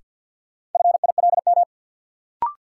Morse code, heard as a keyed tone, about 47 wpm, 700 Hz. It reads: FILM
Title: Sentences from Top 1000 Words - Review Part 3 - New Word Drill - 50wpm